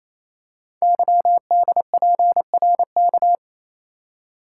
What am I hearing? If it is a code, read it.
Morse code, heard as a keyed tone, about 27 wpm, 700 Hz. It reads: YDPRK
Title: Code Groups - 5 Letters - 28wpm